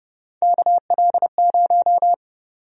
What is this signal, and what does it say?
Morse code, heard as a keyed tone, about 30 wpm, 700 Hz. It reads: KL0
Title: USA Call Sign Prefixes - 30wpm